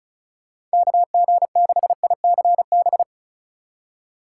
Morse code, heard as a keyed tone, about 34 wpm, 700 Hz. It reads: KG6ICB
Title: USA Call Signs - 6 Characters - 4x Character Spacing - 35wpm